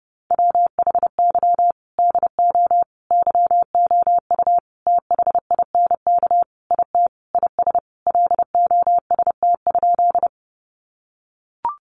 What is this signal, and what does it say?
Morse code, heard as a keyed tone, about 30 wpm, 700 Hz. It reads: WHY DO YOU THINK IT IS LOST?
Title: Sentences from Top 1400 Words 30wpm